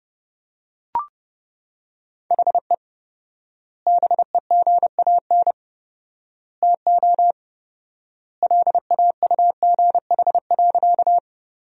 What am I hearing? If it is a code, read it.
Morse code, heard as a keyed tone, about 30 wpm, 700 Hz. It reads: HE BEGAN TO LAUGH.
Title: 4x Word-Spacing - Sentences from Top 500 Words - 30wpm